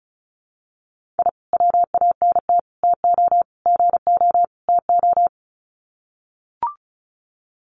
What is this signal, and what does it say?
Morse code, heard as a keyed tone, about 34 wpm, 700 Hz. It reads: I WANT TO GO TO
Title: Sets of 5 Words Encore 35wpm